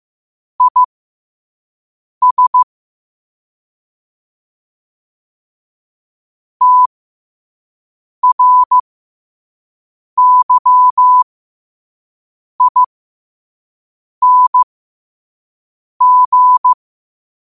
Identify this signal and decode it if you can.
Morse code, heard as a keyed tone, about 15 wpm, spaced as for 5 wpm, 1000 Hz. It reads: IS TRYING
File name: qotdfiles / 5wpm